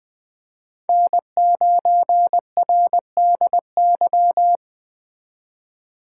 Morse code, heard as a keyed tone, about 20 wpm, 700 Hz. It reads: N9RDY
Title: Speed Racing - USA 1x3 Call Signs 17wpm